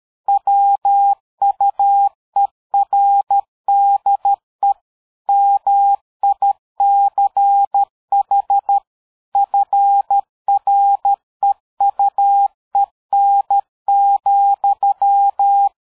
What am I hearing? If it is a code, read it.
Morse code, heard as a keyed tone, about 13 wpm, 800 Hz. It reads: WUERDE MICH FREUEN,
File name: dk5ke / PersWorte 60